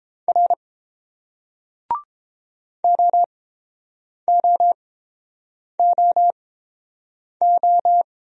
Morse code, heard as a keyed tone, about 29 wpm, spaced as for 7 wpm, 700 Hz. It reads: R OOOO